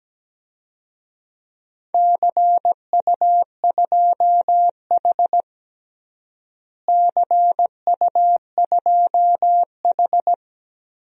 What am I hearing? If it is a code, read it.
Morse code, heard as a keyed tone, about 17 wpm, 700 Hz. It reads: CU2H CU2H
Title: DXCC-100 Call Signs - 4 Characters - Repeated 3x - 17wpm